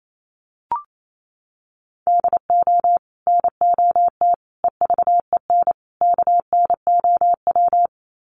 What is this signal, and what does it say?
Morse code, heard as a keyed tone, about 28 wpm, 700 Hz. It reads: DO NOT EVEN KNOW
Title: Sets of 4 Words 28wpm